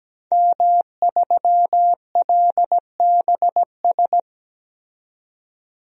Morse code, heard as a keyed tone, about 17 wpm, 700 Hz. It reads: M3LBS
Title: DXCC-100 Call Signs - 5 Characters - 17wpm